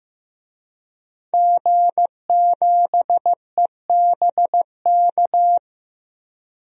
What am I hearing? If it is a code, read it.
Morse code, heard as a keyed tone, about 15 wpm, 700 Hz. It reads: G7EBK
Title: DXCC-100 Call Signs - 5 Characters - Repeated 2x - 15wpm